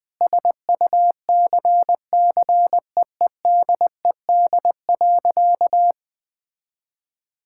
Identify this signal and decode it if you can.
Morse code, heard as a keyed tone, about 21 wpm, 700 Hz. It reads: SUCCEEDED.
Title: Sentences with 7-Word Phrases - 20wpm